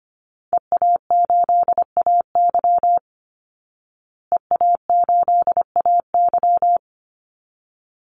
Morse code, heard as a keyed tone, about 25 wpm, 700 Hz. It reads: EA8AY EA8AY
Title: Call Signs Repeated 3x - 25wpm